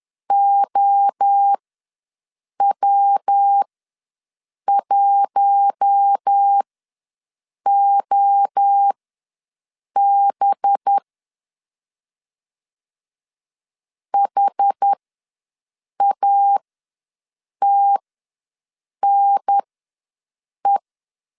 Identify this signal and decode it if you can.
Morse code, heard as a keyed tone, about 10 wpm, spaced as for 6 wpm, 800 Hz. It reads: OW1OB HATNE
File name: LEZIONI / 5 LBVC / 5 LBV ws 10wpm 3645